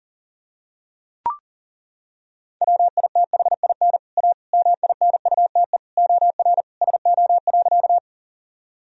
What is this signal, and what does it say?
Morse code, heard as a keyed tone, about 37 wpm, 700 Hz. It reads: WITHIN A MINUTE OR SO.
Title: Sentences from Top 1000 Words - Review Part 4 - 40wpm